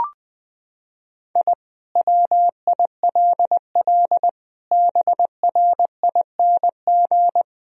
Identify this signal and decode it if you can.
Morse code, heard as a keyed tone, about 19 wpm, 700 Hz. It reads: I WILL BRING
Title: Sentences from Top 1000 Words - Review Part 3 - 20wpm